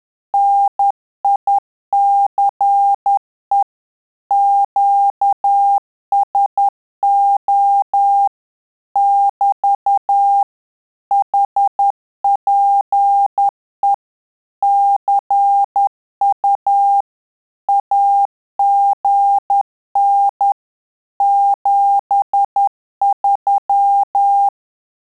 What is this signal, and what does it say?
Morse code, heard as a keyed tone, about 10 wpm, 800 Hz. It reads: NICE QSO <BT> HPE CU AGN 73
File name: Telegraphy-group / QSO CW 10 PPM